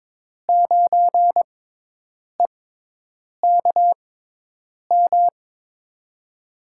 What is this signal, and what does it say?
Morse code, heard as a keyed tone, about 22 wpm, spaced as for 8 wpm, 700 Hz. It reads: 9EKM
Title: USA Call Signs - 6 Characters - 6x Character Spacing - 22wpm